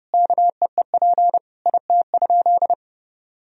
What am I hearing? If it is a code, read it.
Morse code, heard as a keyed tone, about 31 wpm, 700 Hz. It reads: KEEP IT?
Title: Sentences from Top 300 Words and Abbreviations 30wpm